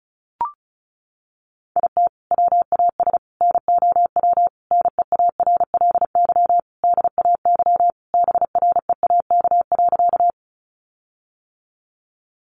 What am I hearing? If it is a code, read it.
Morse code, heard as a keyed tone, about 34 wpm, 700 Hz. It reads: IT WAS NOW NEARLY DAY BREAK.
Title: Sentences from Top 1300 Words 35wpm